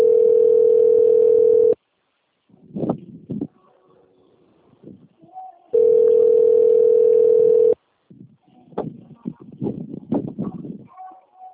Ringback tone (North America), 440 + 480 Hz.